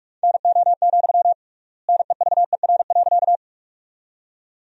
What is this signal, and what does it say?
Morse code, heard as a keyed tone, about 46 wpm, 700 Hz. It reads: NO, NEVER.
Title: 3x Word-Spacing - Sentences from Top 300 Words - 45wpm